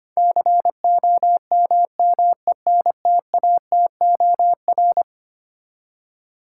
Morse code, heard as a keyed tone, about 25 wpm, 700 Hz. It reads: COMMENTATOR
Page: morse-code-ninja-practice-25wpm